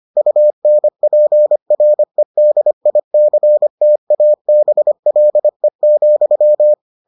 Morse code, heard as a keyed tone, about 25 wpm, 600 Hz. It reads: UNPREDICTABLE,